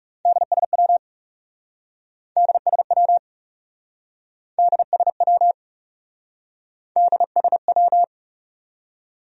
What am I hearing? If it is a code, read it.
Morse code, heard as a keyed tone, about 36 wpm, 700 Hz. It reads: DSW DSW DSW DSW
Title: Speed-Racing Abbreviations - 30wpm